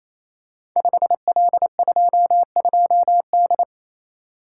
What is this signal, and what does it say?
Morse code, heard as a keyed tone, about 29 wpm, 700 Hz. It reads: 5L22D
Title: Code Groups - 5 Letter:Numbers - 28wpm